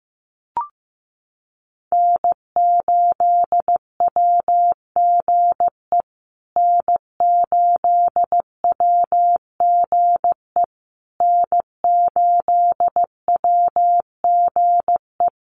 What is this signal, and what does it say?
Morse code, heard as a keyed tone, about 15 wpm, 700 Hz. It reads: N8WGE N8WGE N8WGE
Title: USA Call Signs Repeated 3x 15wpm